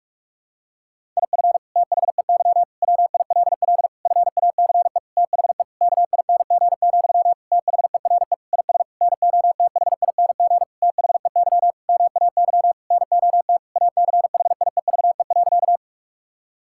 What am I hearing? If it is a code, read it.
Morse code, heard as a keyed tone, about 46 wpm, 700 Hz. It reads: IF THEY WILL FACE THE KING, THERE IS NOTHING THEY MAY NOT ACHIEVE.
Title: Sentences from Top 1000 Words - Review Part 6 - 45wpm